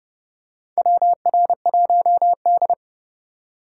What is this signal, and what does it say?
Morse code, heard as a keyed tone, about 30 wpm, 700 Hz. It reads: WR1D
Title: USA Call Signs - 4x Character Spacing - 30wpm